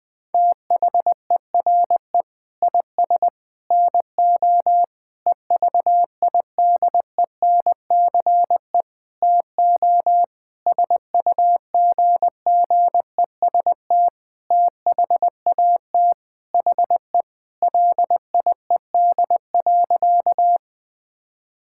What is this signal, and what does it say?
Morse code, heard as a keyed tone, about 20 wpm, 700 Hz. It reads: THERE IS NO EVIDENCE TO SUGGEST THAT HE LIED.
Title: Sentences with 7-Word Phrases - 20wpm